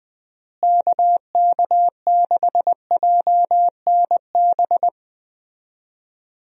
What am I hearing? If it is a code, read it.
Morse code, heard as a keyed tone, about 20 wpm, 700 Hz. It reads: KK6JNB
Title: USA Call Signs - 6 Characters - Repeated 3x - 20wpm